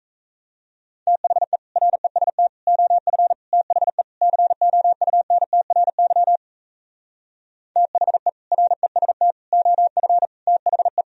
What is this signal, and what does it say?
Morse code, heard as a keyed tone, about 37 wpm, 700 Hz. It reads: THE REST OF THE COUNTRY THE REST OF THE